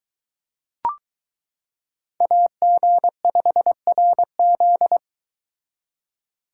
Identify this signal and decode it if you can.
Morse code, heard as a keyed tone, about 22 wpm, 700 Hz. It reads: AG5RZ